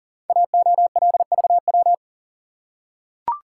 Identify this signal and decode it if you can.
Morse code, heard as a keyed tone, about 38 wpm, 700 Hz. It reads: AOLVW
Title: Code Groups - 5 Letters - 40wpm